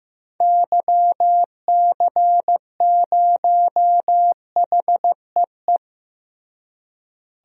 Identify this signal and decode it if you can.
Morse code, heard as a keyed tone, about 16 wpm, 700 Hz. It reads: YC0HEE
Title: DXCC-100 Call Signs - 6 Characters - Repeated 2x - 15wpm